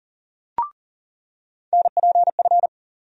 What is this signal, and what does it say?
Morse code, heard as a keyed tone, about 40 wpm, 700 Hz. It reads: NPF
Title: USA Call Sign Postfixes - 3 Letters 40wpm